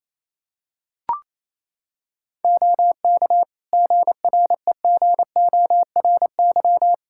Morse code, heard as a keyed tone, about 28 wpm, 700 Hz. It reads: OK GREGORY
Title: Practice QSO - Exchange 3 - More Info - 28wpm